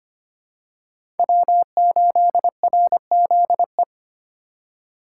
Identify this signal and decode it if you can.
Morse code, heard as a keyed tone, about 25 wpm, 700 Hz. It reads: W8RZE
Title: USA Call Signs - 5 Characters - 4x Character Spacing - 25wpm